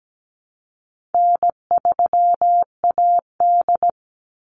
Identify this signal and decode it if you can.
Morse code, heard as a keyed tone, about 17 wpm, 700 Hz. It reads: N3AD
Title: Easy Call Signs Repeated 3x - 17wpm